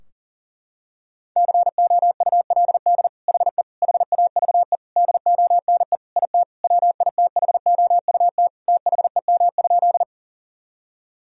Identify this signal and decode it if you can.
Morse code, heard as a keyed tone, about 40 wpm, 700 Hz. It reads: COULD HE HAVE DONE IT WITHOUT THEM?
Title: Sentences from Top 1000 Words - Review Part 3 - 40wpm